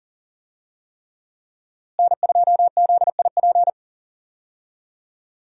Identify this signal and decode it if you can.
Morse code, heard as a keyed tone, about 38 wpm, 690 Hz. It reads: N2ZIP
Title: USA Call Signs - 5 Characters - Repeated 3x - 40wpm